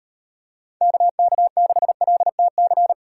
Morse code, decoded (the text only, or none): KK6LTC